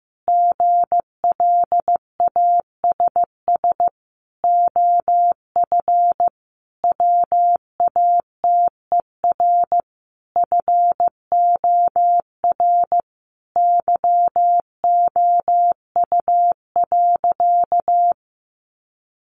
Morse code, heard as a keyed tone, about 14 wpm, 700 Hz. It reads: GLASS OF WATER FOR YOU.
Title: Sentences from Top 1300 Words 15wpm